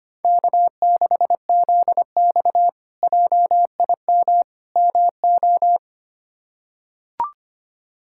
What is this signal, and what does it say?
Morse code, heard as a keyed tone, about 25 wpm, 700 Hz. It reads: K6ZX JIM MO
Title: K1USN SST QSO Practice - 25wpm